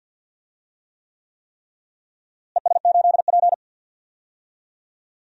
Morse code, heard as a keyed tone, about 51 wpm, 700 Hz. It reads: ES8P
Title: DXCC-100 Call Signs - 4 Characters - Repeated 3x - 50wpm